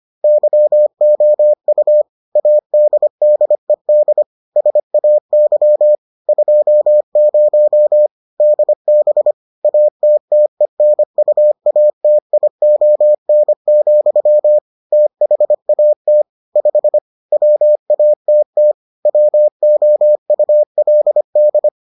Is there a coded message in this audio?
Yes: Morse code from 0.2 to 21.7 s, heard as a keyed tone, about 25 wpm, 600 Hz. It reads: YOU ADDED SAY 20 DB ATTENUATION, THAT 5 WATT WOULD